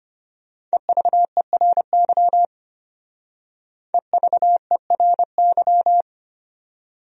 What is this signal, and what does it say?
Morse code, heard as a keyed tone, about 32 wpm, 700 Hz. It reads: EVERY EVERY